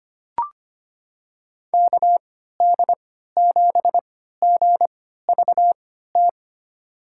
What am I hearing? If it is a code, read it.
Morse code, heard as a keyed tone, about 25 wpm, spaced as for 14 wpm, 700 Hz. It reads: KD7GVT
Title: USA Call Signs - 6 Characters - 3x Character Spacing - 25wpm